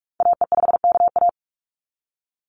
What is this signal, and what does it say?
Morse code, heard as a keyed tone, about 44 wpm, 700 Hz. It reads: AE5KA
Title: NA SOTA Activator Callsigns 45wpm